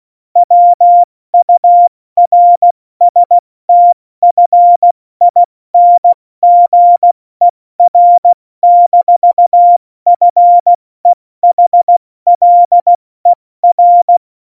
Morse code, heard as a keyed tone, about 16 wpm, 700 Hz. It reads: WURSTFINGER-FEHLER
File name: Morsekurs / Witze / Witz 22. Januar 2019, 70 F80